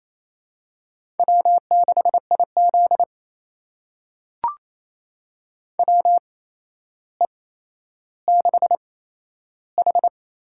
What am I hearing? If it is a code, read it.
Morse code, heard as a keyed tone, about 28 wpm, spaced as for 12 wpm, 700 Hz. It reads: W6IZ WE6H
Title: USA Call Signs - 4 Characters - 8x Character Spacing - 28wpm